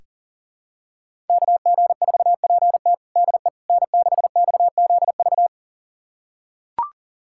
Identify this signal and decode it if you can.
Morse code, heard as a keyed tone, about 40 wpm, 700 Hz. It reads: KG4PT DE N6XZV
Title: USA Callsign DE Callsign - 5 Characters - 40wpm